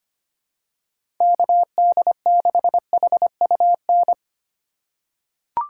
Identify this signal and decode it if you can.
Morse code, heard as a keyed tone, about 25 wpm, 700 Hz. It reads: KD6HUN
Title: USA Call Signs - 6x Character Spacing - 25wpm